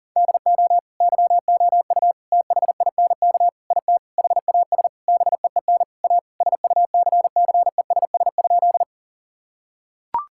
Morse code, heard as a keyed tone, about 40 wpm, 700 Hz. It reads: DO YOU THINK IT HAS BEEN A SUCCESS?
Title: Sentences from Top 1000 Words - Review Part 6 - 40wpm